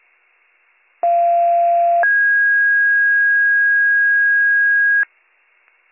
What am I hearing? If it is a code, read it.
Two-tone page, heard as a one-second tone, then a three-second tone, about 690 then 1650 Hz. It is heard from 1.0 to 5.0 s.